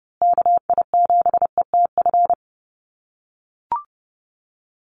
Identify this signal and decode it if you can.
Morse code, heard as a keyed tone, about 30 wpm, 700 Hz. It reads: KI7ETF
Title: USA Call Signs Repeated 3x 30wpm